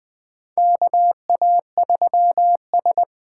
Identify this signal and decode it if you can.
Morse code, heard as a keyed tone, about 20 wpm, 700 Hz. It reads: KA3S
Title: USA Call Signs - 6 Characters - 8x Character Spacing - 20wpm